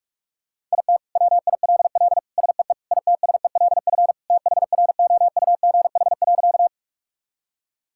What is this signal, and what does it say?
Morse code, heard as a keyed tone, about 42 wpm, 700 Hz. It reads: IT WILL SEE ITSELF THROUGH.
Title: Sentences from Top 1000 Words - Review Part 5 - 45wpm